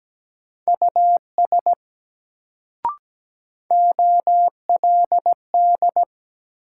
Morse code, heard as a keyed tone, about 17 wpm, 700 Hz. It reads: US OLD
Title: Sentences from Top 200 Words - New Word Drill - 17wpm